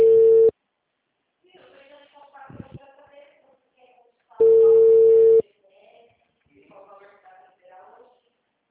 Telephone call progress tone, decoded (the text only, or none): ringback tone